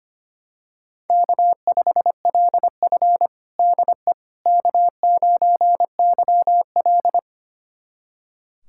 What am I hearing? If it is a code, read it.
Morse code, heard as a keyed tone, about 25 wpm, 700 Hz. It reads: K5LF DE K9YL
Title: USA Callsign DE Callsign - 4 Characters - 25wpm